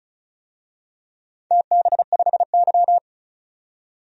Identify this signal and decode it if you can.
Morse code, heard as a keyed tone, about 36 wpm, 700 Hz. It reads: TB5Y
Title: DXCC-100 Call Signs - 4 Characters - 35wpm